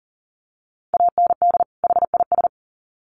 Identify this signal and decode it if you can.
Morse code, heard as a keyed tone, about 40 wpm, 700 Hz. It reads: AND HIS
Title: Sets of 2 Words 40wpm